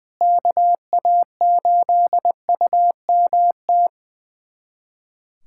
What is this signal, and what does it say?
Morse code, heard as a keyed tone, about 20 wpm, 700 Hz. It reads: KA8UMT
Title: USA Callsign DE Callsign - 6 Characters - 20wpm